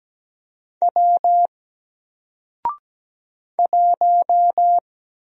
Morse code, heard as a keyed tone, about 17 wpm, 700 Hz. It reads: W 1
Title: Single Letter-Number Alt 17wpm